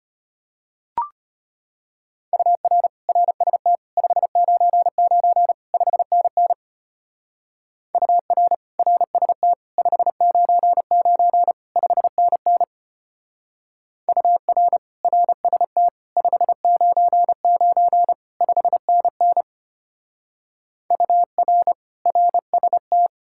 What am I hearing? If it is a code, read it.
Morse code, heard as a keyed tone, about 32 wpm, 700 Hz. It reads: UR RST 599 5NN UR RST 599 5NN UR RST 599 5NN UR RST